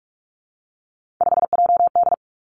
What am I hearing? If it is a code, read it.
Morse code, heard as a keyed tone, about 46 wpm, 700 Hz. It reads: 5JD
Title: Three Challenging Letter-Number Combinations 45wpm